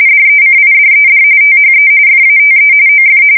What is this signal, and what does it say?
RTTY, about 45.45 baud. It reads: DECIDED THE FATE OF
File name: fsk170 / battleshipnj